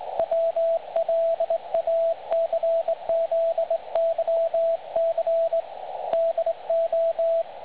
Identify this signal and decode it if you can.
Morse code, heard as a keyed tone, about 20 wpm, 660 Hz, with radio noise. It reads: WLACZYC DO
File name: dlazaawansowanych / nagraniazpasma / websdr recording 3565.0kHz